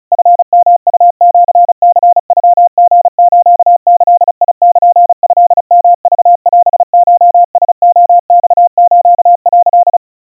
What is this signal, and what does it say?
Morse code (Japanese Wabun code), heard as a keyed tone, about 36 wpm, 700 Hz. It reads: チヨウシニノリスギルトヨクオコラレマス」
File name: kawa / Morse sample / kamyu 130 cpm 700 Hz Crispy